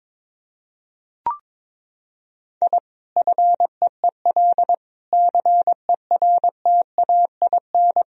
Morse code, heard as a keyed tone, about 22 wpm, 700 Hz. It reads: I FEEL CERTAIN
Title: Sentences from Top 1000 Words - Review Part 2 - 22wpm